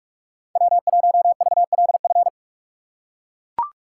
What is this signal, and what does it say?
Morse code, heard as a keyed tone, about 42 wpm, 700 Hz. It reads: W1VLF